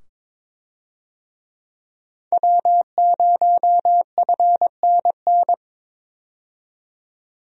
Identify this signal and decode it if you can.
Morse code, heard as a keyed tone, about 22 wpm, 700 Hz. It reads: W0FNN
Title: USA Call Signs - 5 Characters - 22wpm